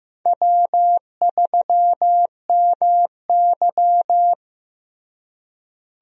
Morse code, heard as a keyed tone, about 16 wpm, 700 Hz. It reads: W3MY
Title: USA Call Signs - 4 Characters - Repeated 3x - 15wpm